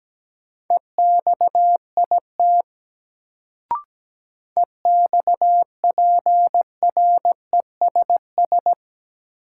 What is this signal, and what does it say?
Morse code, heard as a keyed tone, about 17 wpm, 700 Hz. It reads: EXIT EXPRESS
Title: Sentences from Top 1100 Words - New Word Drill - 17wpm